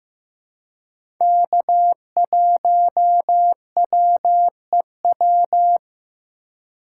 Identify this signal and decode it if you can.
Morse code, heard as a keyed tone, about 15 wpm, 700 Hz. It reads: K1WEW
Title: USA Call Signs - 5 Characters - 2x Character Spacing - 15wpm